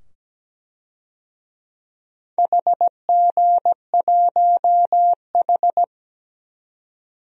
Morse code, heard as a keyed tone, about 17 wpm, 700 Hz. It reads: HG1H
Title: DXCC-100 Call Signs - 4 Characters - 17wpm